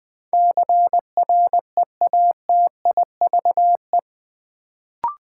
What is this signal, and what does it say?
Morse code, heard as a keyed tone, about 20 wpm, 700 Hz. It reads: CREATIVE
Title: Sentences from Top 1000 Words - Review Part 5 - New Word Drill - 20wpm